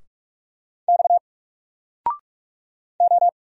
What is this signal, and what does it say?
Morse code, heard as a keyed tone, about 46 wpm, 700 Hz. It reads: X O
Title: Single Letter-Number Alt 45wpm